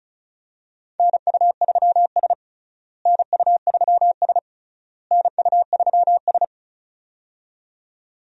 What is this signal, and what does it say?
Morse code, heard as a keyed tone, about 34 wpm, 700 Hz. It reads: NU3S NU3S NU3S
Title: USA Call Signs - 4 Characters - Repeated 3x - 35wpm